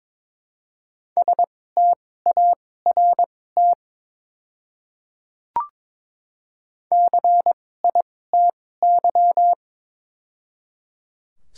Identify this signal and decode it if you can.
Morse code, heard as a keyed tone, about 22 wpm, spaced as for 16 wpm, 700 Hz. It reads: START CITY
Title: 2x Character Spacing - Top 300 Words - 22wpm